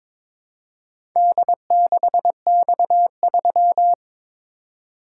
Morse code, heard as a keyed tone, about 22 wpm, 700 Hz. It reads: D6X3
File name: practice / 22wpm / Four Letter-Number Combinations - 22wpm